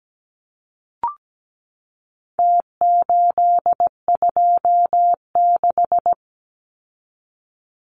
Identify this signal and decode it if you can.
Morse code, heard as a keyed tone, about 17 wpm, 700 Hz. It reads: T826
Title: Numbers - Contest Style - 4-Digit 17wpm